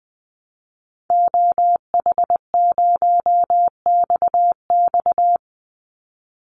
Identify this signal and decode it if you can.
Morse code, heard as a keyed tone, about 20 wpm, 700 Hz. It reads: OH0XX
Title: Call Signs Repeated 3x - 20wpm